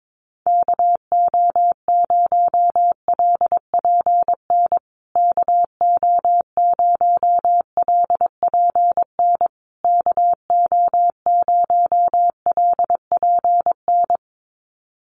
Morse code, heard as a keyed tone, about 22 wpm, 700 Hz. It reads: KO0LPN KO0LPN KO0LPN
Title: USA Call Signs Repeated 3x 22wpm